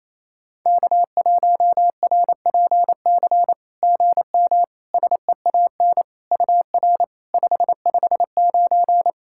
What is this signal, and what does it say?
Morse code, heard as a keyed tone, about 28 wpm, 700 Hz. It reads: K1RPC GM SEAN UR 559
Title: SOTA QSO Practice 28wpm